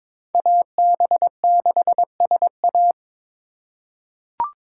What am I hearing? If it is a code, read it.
Morse code, heard as a keyed tone, about 22 wpm, 700 Hz. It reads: AB6SA
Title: USA Call Signs - 5 Characters - 2x Character Spacing - 22wpm